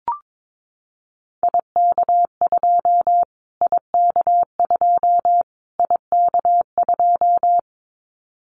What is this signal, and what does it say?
Morse code, heard as a keyed tone, about 22 wpm, 700 Hz. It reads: IK2 IK2 IK2
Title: Callsign Prefixes for Top 25 DXCC - Repeated 3x - 22wpm